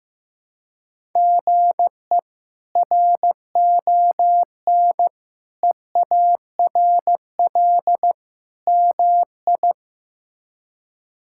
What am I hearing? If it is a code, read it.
Morse code, heard as a keyed tone, about 15 wpm, 700 Hz. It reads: GE RON EARL MI
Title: K1USN SST QSO Practice - 15wpm